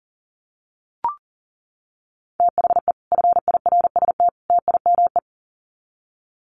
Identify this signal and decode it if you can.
Morse code, heard as a keyed tone, about 40 wpm, 700 Hz. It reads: THE FIRST TIME
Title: Sets of 3 Words 40wpm